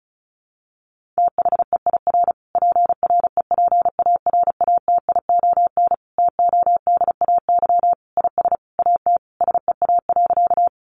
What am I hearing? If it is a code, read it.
Morse code, heard as a keyed tone, about 34 wpm, 700 Hz. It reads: THEIR PREPARATION TODAY IS AT SEA.
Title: Sentences from Top 1400 Words 35wpm